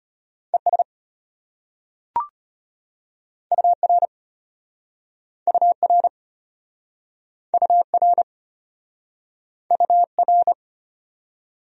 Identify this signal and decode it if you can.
Morse code, heard as a keyed tone, about 33 wpm, 700 Hz. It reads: ES UR UR UR UR